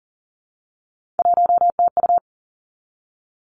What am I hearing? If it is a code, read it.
Morse code, heard as a keyed tone, about 37 wpm, 700 Hz. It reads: 1TU
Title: Three Challenging Letter-Number Combinations 40wpm